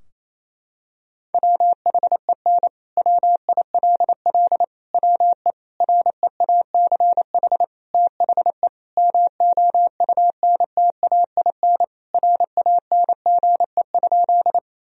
Morse code, heard as a keyed tone, about 28 wpm, 700 Hz. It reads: WHEN WILL WE REACH THE MOUNTAIN RANGE?
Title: Sentences from Top 1000 Words - Review Part 4 - 28wpm